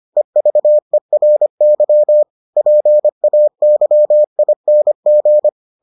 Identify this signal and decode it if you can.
Morse code, heard as a keyed tone, about 25 wpm, 600 Hz. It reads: EVERY PAYING